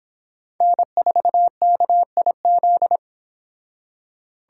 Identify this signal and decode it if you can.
Morse code, heard as a keyed tone, about 28 wpm, 700 Hz. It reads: N4KIZ